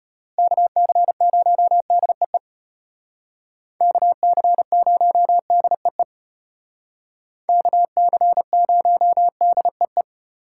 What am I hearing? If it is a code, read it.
Morse code, heard as a keyed tone, about 34 wpm, 700 Hz. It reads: KC0DEE KC0DEE KC0DEE